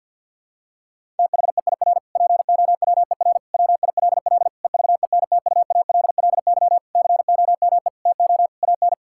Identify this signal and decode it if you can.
Morse code, heard as a keyed tone, about 50 wpm, 700 Hz. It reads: THEIR POWER WILL EVENTUALLY COME TO AN
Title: Sentences from Top 1000 Words - Review Part 6 - 50wpm